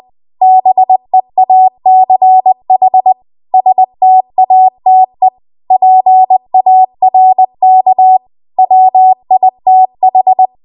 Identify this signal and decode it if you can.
Morse code, heard as a keyed tone, about 20 wpm, 750 Hz. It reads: BEACH STATE PARK WITH
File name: Morse / 20WPM